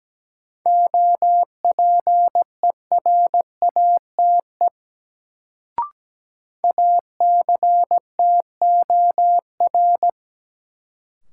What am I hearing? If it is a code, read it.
Morse code, heard as a keyed tone, about 17 wpm, 700 Hz. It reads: OPERATE ACTOR
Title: Sentences from Top 1500 Words - New Word Drill - 17wpm